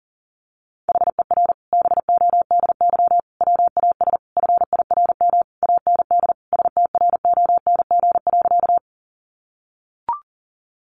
Morse code, heard as a keyed tone, about 40 wpm, 700 Hz. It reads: HER BODY WAS FIRM AND STRONG.
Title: Sentences from Top 1400 Words 40wpm